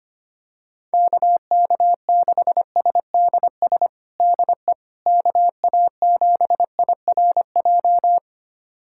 Morse code, heard as a keyed tone, about 25 wpm, 700 Hz. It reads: KK6SDS DE KA7IRJ
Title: USA Callsign DE Callsign - 6 Characters - 25wpm